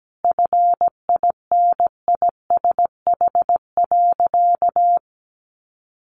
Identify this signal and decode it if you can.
Morse code, heard as a keyed tone, about 17 wpm, 700 Hz. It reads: FINISH.
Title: Sentences from Top 1200 Words 17wpm